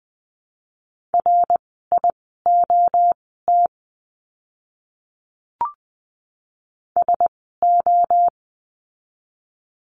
Morse code, heard as a keyed tone, about 20 wpm, spaced as for 14 wpm, 700 Hz. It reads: RIOT SO
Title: 061 Lesson 11f - Letter h - Words 20wpm